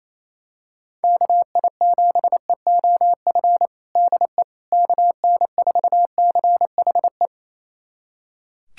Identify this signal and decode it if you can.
Morse code, heard as a keyed tone, about 28 wpm, 700 Hz. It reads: KI7EOF DE KN4CHE
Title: USA Callsign DE Callsign - 6 Characters - 28wpm